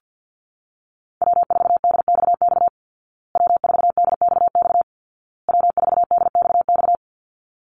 Morse code, heard as a keyed tone, about 48 wpm, 700 Hz. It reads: W4DXX W4DXX W4DXX
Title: Call Signs Repeated 2x - 50wpm